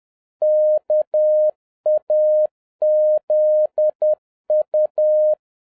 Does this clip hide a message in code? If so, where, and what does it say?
Yes: Morse code from 0.4 to 5.3 s, heard as a keyed tone, about 10 wpm, 610 Hz. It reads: KAZU